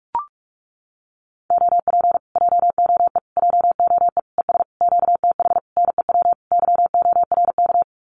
Morse code, heard as a keyed tone, about 44 wpm, 700 Hz. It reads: OP JOE JOE ES QTH NEW YORK
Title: Common QSO Phrases Encore 45wpm